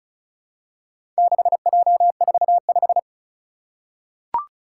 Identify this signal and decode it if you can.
Morse code, heard as a keyed tone, about 36 wpm, 700 Hz. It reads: 6J45
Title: Four Challenging Letter-Number Combinations 35wpm